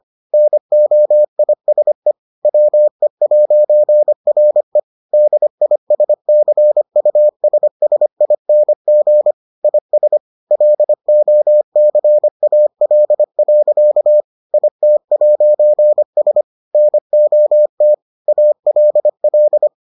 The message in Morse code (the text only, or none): NOISE WE'RE DISCUSSING IS LOCAL. IT'S NOT ALL